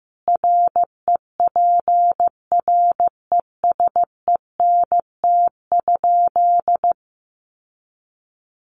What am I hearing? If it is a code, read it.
Morse code, heard as a keyed tone, about 15 wpm, 700 Hz. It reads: REPRESENT?
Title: Sentences from Top 1200 Words 15wpm